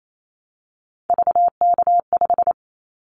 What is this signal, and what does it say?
Morse code, heard as a keyed tone, about 28 wpm, 700 Hz. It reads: VK5
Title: Callsign Prefixes for Top 50 DXCC - 28wpm